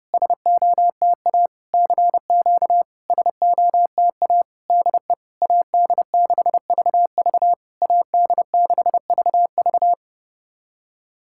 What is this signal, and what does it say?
Morse code, heard as a keyed tone, about 30 wpm, 700 Hz. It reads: SOTA CQ SOTA DE AD6VV AD6VV
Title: SOTA QSO Practice 30wpm